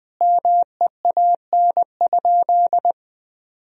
Morse code, heard as a keyed tone, about 20 wpm, 700 Hz. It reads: MEAN?
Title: Sentences from Top 1000 Words - Review Part 1 - 20wpm